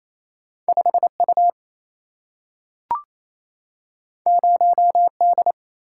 Morse code, heard as a keyed tone, about 28 wpm, 700 Hz. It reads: 5U 0D
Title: Single-Letter Single-Number 28wpm